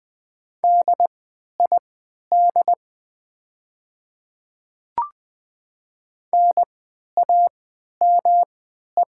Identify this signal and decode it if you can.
Morse code, heard as a keyed tone, about 20 wpm, spaced as for 11 wpm, 690 Hz. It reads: DID NAME
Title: 3x Character Spacing - Top 300 Words - 20wpm